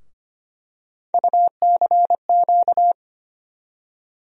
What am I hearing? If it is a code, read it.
Morse code, heard as a keyed tone, about 25 wpm, 700 Hz. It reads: UCQ
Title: USA Call Sign Postfixes - 3 Letters 25wpm